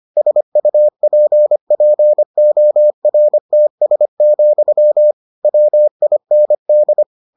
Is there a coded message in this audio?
Yes: Morse code from 0.2 to 7.0 s, heard as a keyed tone, about 25 wpm, 600 Hz. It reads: SUPPORTS, WIND